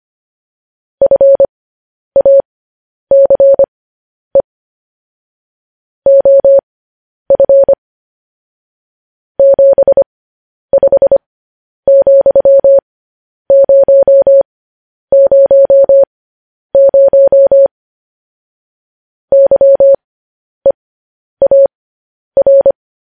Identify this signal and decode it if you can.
Morse code, heard as a keyed tone, about 25 wpm, spaced as for 10 wpm, 550 Hz. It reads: FACE OF 75,000 YEAR